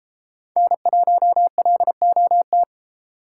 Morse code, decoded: N1LOT

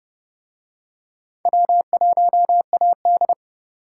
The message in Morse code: W1AD

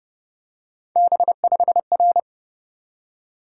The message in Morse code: B5R

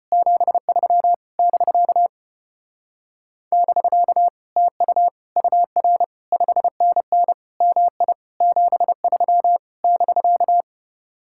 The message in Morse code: 73 <BK> <BK> TU UR 5NN MI 73 <BK>

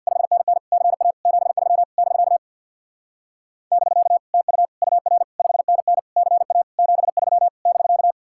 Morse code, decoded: 5NN CA 73 <BK> <BK> TU UR 5NN CA 73 <BK>

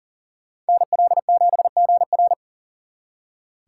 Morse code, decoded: NL7GR

